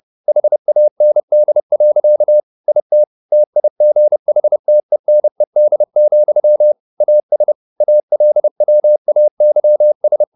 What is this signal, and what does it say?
Morse code, heard as a keyed tone, about 30 wpm, 600 Hz. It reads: HAND. IT TIGHTENED, AS ALWAYS